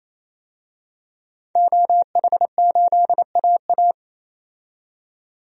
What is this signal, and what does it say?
Morse code, heard as a keyed tone, about 28 wpm, 700 Hz. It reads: OH8AA
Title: DXCC-100 Call Signs - 5 Characters - 28wpm